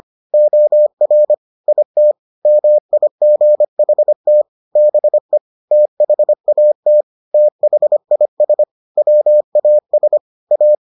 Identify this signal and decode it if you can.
Morse code, heard as a keyed tone, about 25 wpm, 600 Hz. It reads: OR IT MIGHT BE THAT THIS WAS A